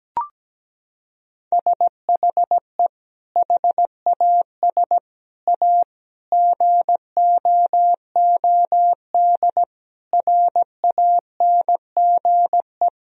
Morse code, heard as a keyed tone, about 17 wpm, 700 Hz. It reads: SHE HAS A GOOD RANGE
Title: Sentences from Top 1000 Words - Review Part 4 - 17wpm